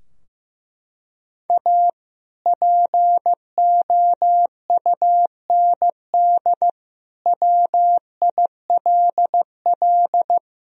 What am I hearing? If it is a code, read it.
Morse code, heard as a keyed tone, about 15 wpm, 700 Hz. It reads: A POUND WILL
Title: Sentences from Top 1000 Words - Review Part 2 - 15wpm